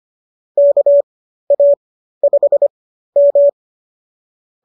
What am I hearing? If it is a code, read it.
Morse code, heard as a keyed tone, about 25 wpm, spaced as for 13 wpm, 570 Hz. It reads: KA5M